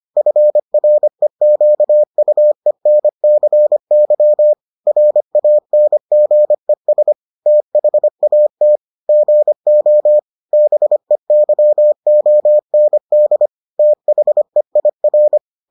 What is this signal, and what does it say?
Morse code, heard as a keyed tone, about 25 wpm, 600 Hz. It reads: FREQUENCY RANGES THAT GO BEYOND THEIR